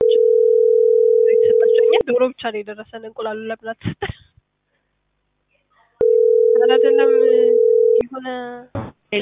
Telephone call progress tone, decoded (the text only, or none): ringback tone